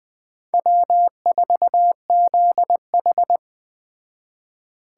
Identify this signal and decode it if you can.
Morse code, heard as a keyed tone, about 21 wpm, 700 Hz. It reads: W4ZH